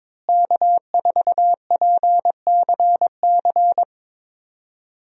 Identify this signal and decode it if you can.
Morse code, heard as a keyed tone, about 21 wpm, 700 Hz. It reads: K4PCC
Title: USA Call Signs - 5 Characters - Repeated 3x - 22wpm